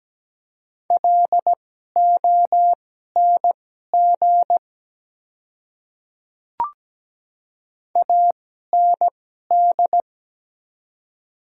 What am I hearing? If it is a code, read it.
Morse code, heard as a keyed tone, about 17 wpm, spaced as for 12 wpm, 700 Hz. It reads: LONG AND
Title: 2x Character Spacing - Top 100 Words - 17wpm